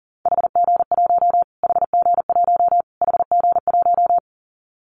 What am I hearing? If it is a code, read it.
Morse code, heard as a keyed tone, about 40 wpm, 700 Hz. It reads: HG1 HG1 HG1
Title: Callsign Prefixes for Top 25 DXCC - Repeated 3x - 40wpm